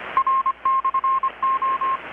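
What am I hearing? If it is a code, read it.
Morse code, heard as a keyed tone, about 25 wpm, 1060 Hz. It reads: R/O
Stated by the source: Morse from KPH's Night of Nights XX maritime radiotelegraphy event, received in Chiapas, Mexico about 3,800 km away on a Kenwood R-600 with a grounded 10 m long-wire antenna.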